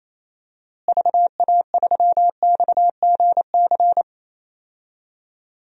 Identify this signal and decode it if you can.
Morse code, heard as a keyed tone, about 28 wpm, 700 Hz. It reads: VA3XGC
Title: DXCC-100 Call Signs - 6 Characters - Repeated 3x - 28wpm